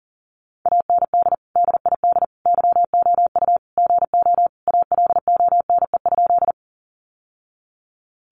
Morse code, heard as a keyed tone, about 40 wpm, 700 Hz. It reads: AND DID YOU GO ALONE?
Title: Sentences from Top 1000 Words 40wpm